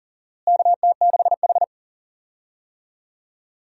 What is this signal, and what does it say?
Morse code, heard as a keyed tone, about 37 wpm, 700 Hz. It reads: KT6H
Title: USA Call Signs - 4 Characters - Repeated 3x - 40wpm